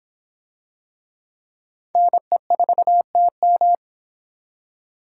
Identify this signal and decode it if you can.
Morse code, heard as a keyed tone, about 23 wpm, 700 Hz. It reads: NE4TM